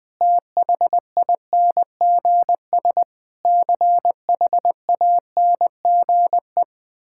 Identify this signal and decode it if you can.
Morse code, heard as a keyed tone, about 20 wpm, 700 Hz. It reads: THINGS CHANGE
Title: Sentences with 8-Word Phrases - Spoken Beginning - 20wpm